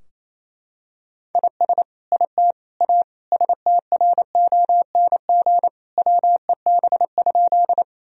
Morse code, heard as a keyed tone, about 28 wpm, 700 Hz. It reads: IS IT A STRONG WEB?
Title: Sentences from Top 1000 Words - Review Part 4 - 28wpm